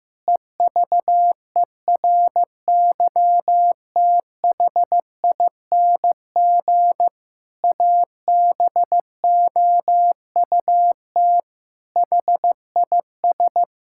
Morse code, heard as a keyed tone, about 15 wpm, 700 Hz. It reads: EVERYTHING ABOUT HIS
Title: Pangram Challenge v2 - 15wpm